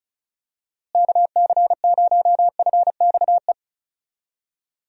Morse code, heard as a keyed tone, about 36 wpm, 700 Hz. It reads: KC0FXE
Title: USA Call Signs - 6 Characters - Repeated 2x - 35wpm